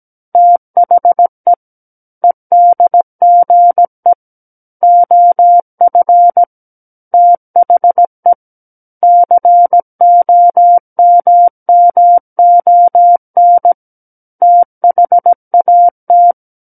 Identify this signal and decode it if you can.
Morse code, heard as a keyed tone, about 17 wpm, 700 Hz. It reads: THE EDGE OF THE COMMON THAT
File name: Book One / 08-Chapter 8 - 17 WPM